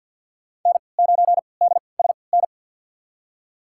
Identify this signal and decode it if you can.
Morse code, heard as a keyed tone, about 53 wpm, spaced as for 29 wpm, 700 Hz. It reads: N9DSN
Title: USA Call Signs - 5 Characters - 3x Character Spacing - 50wpm